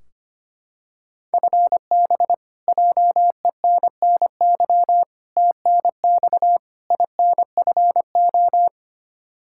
Morse code, heard as a keyed tone, about 25 wpm, 700 Hz. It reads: FB JENNY TNX INFO
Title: K1USN SST QSO Practice 25wpm